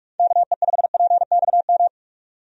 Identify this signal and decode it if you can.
Morse code, heard as a keyed tone, about 46 wpm, 700 Hz. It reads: KE5PXM
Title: USA Call Signs - 6 Characters - Repeated 3x - 45wpm